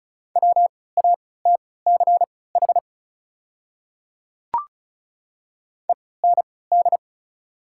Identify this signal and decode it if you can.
Morse code, heard as a keyed tone, about 36 wpm, spaced as for 20 wpm, 690 Hz. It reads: WATCH END
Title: 3x Character Spacing - Top 300 Words - 35wpm